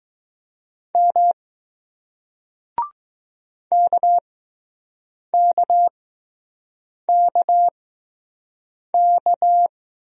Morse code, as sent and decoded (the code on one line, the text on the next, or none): -- / -.- -.- -.- -.-
M KKKK